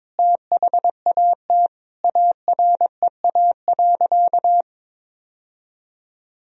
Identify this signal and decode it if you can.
Morse code, heard as a keyed tone, about 22 wpm, 700 Hz. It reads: THAT AREA.
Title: Sentences from Top 1000 Words - Review Part 2 - 22wpm